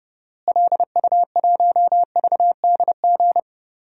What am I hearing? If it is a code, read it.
Morse code, heard as a keyed tone, about 30 wpm, 700 Hz. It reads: LU1VDG
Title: DXCC-100 Call Signs - 6 Characters - Repeated 2x - 30wpm